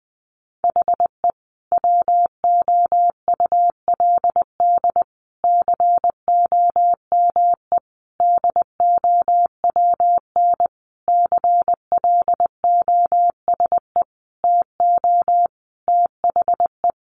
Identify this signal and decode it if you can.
Morse code, heard as a keyed tone, about 20 wpm, 700 Hz. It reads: HE WOULD COME DOWN CLOSE TO THE
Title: Sentences from Top 1300 Words 20wpm